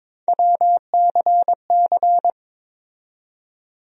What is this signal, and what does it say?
Morse code, heard as a keyed tone, about 21 wpm, 700 Hz. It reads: WCC
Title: USA Call Sign Postfixes - 3 Letters 22wpm